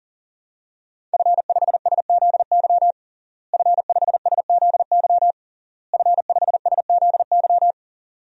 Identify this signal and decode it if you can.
Morse code, heard as a keyed tone, about 40 wpm, 700 Hz. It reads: F5SZY F5SZY F5SZY
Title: DXCC-100 Call Signs - 5 Characters - Repeated 3x - 40wpm